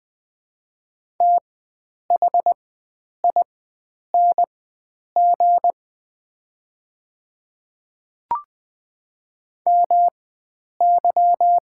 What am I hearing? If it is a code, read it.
Morse code, heard as a keyed tone, about 20 wpm, spaced as for 9 wpm, 700 Hz. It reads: THING MY
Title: 4x Character Spacing - Top 100 Words - 20wpm